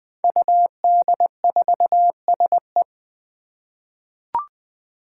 Morse code, heard as a keyed tone, about 20 wpm, 700 Hz. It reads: UD4SE
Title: Five Challenging Letter-Number Combinations 20wpm